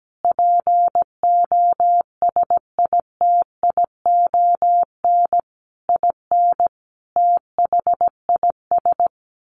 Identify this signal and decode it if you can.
Morse code, heard as a keyed tone, about 17 wpm, 700 Hz. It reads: POSITION IN THIS
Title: Sentences from Top 1200 Words 17wpm